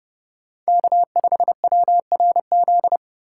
Morse code, heard as a keyed tone, about 30 wpm, 700 Hz. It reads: K5WRZ
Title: USA Call Signs - 5 Characters - 4x Character Spacing - 30wpm